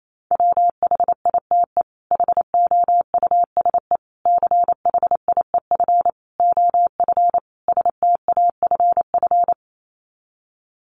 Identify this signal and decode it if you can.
Morse code, heard as a keyed tone, about 28 wpm, 700 Hz. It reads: WHITE HOUSE CHIEF OF STAFF
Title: Sets of 5 Words Encore 28wpm